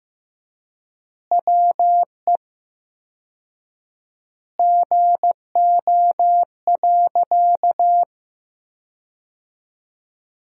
Morse code, heard as a keyed tone, about 15 wpm, 700 Hz. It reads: WE GO.
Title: 4x Word-Spacing - Sentences from Top 200 Words - 15wpm